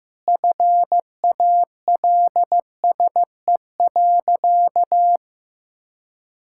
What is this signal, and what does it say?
Morse code, heard as a keyed tone, about 15 wpm, 700 Hz. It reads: FALSE.
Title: Sentences from Top 1600 Words 15wpm